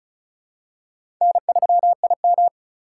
Morse code, heard as a keyed tone, about 36 wpm, 700 Hz. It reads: N3IM
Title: USA Call Signs - 3x Character Spacing - 35wpm